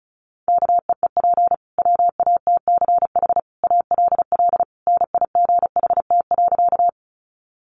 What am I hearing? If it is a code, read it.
Morse code, heard as a keyed tone, about 34 wpm, 700 Hz. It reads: KEEP WATCH ALL NIGHT.
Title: Sentences from Top 300 Words 35wpm